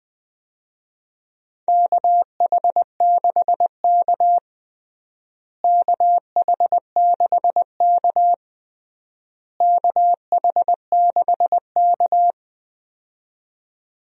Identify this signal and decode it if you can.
Morse code, heard as a keyed tone, about 20 wpm, 700 Hz. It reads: KH6K KH6K KH6K
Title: DXCC-100 Call Signs - 4 Characters - Repeated 3x - 20wpm